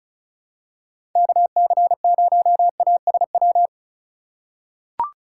Morse code, heard as a keyed tone, about 34 wpm, 690 Hz. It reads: KC0ASW
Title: USA Call Signs - 6 Characters - 2x Character Spacing - 35wpm